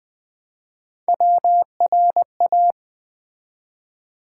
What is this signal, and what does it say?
Morse code, heard as a keyed tone, about 20 wpm, 700 Hz. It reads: WRA
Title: Build A Word - 1 Letter At A Time - 3 to 5-Letter Words - 20wpm